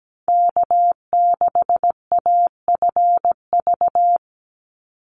Morse code, heard as a keyed tone, about 17 wpm, 700 Hz. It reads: K6AFV
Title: USA Call Signs 17wpm